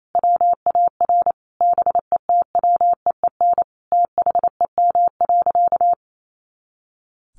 Morse code, heard as a keyed tone, about 28 wpm, 700 Hz. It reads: WAR BETWEEN THEM.
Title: Sentences from Top 400 Words 28wpm